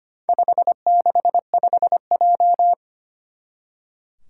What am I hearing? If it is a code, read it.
Morse code, heard as a keyed tone, about 25 wpm, 700 Hz. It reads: 565J
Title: Four Challenging Letter-Number Combinations 25wpm